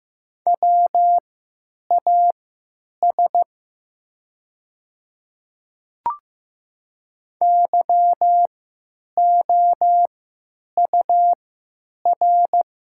Morse code, heard as a keyed tone, about 15 wpm, spaced as for 9 wpm, 700 Hz. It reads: WAS YOUR
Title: 3x Character Spacing - Top 300 Words - 15wpm